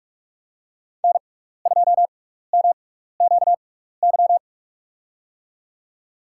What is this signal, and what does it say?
Morse code, heard as a keyed tone, about 45 wpm, spaced as for 15 wpm, 700 Hz. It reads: N2MQY
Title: USA Call Signs - 5 Characters - 6x Character Spacing - 45wpm